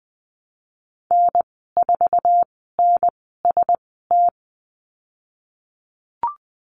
Morse code, heard as a keyed tone, about 19 wpm, spaced as for 14 wpm, 700 Hz. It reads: N4NST
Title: 055 Lesson 10g - Letter r - Call Signs 20wpm